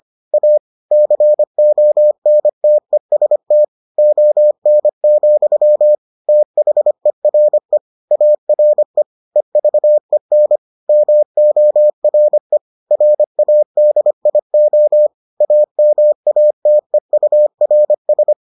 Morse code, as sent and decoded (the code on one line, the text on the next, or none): .- / -.-. --- -. - . ... - / --- -. --..-- / - .... .-. / .- .-. / . ...- . -. / -- --- .-. / .-. .- -.. .. --- / .- -- .- - . ..- .-. ...
A CONTEST ON, THERE ARE EVEN MORE RADIO AMATEURS